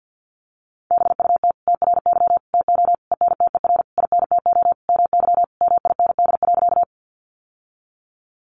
Morse code, heard as a keyed tone, about 50 wpm, 700 Hz. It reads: BUT TRY TO ENTER INTO MY MIND.
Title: Sentences from Top 1300 Words 50wpm